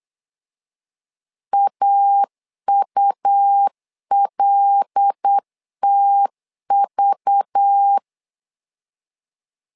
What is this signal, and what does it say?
Morse code, heard as a keyed tone, about 8 wpm, 800 Hz. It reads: AULTV
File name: LEZIONI / 5 LBVC / 5 LBV 8wpm 2837